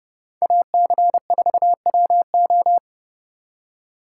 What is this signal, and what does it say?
Morse code, heard as a keyed tone, about 35 wpm, 700 Hz. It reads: AC4WO
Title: Speed Racing - USA 2x2 Call Signs 30wpm